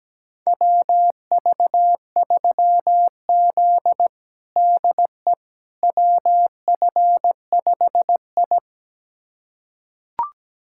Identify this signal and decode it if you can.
Morse code, heard as a keyed tone, about 17 wpm, 700 Hz. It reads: WV3Z DE WF5I
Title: USA Callsign DE Callsign - 4 Characters - 17wpm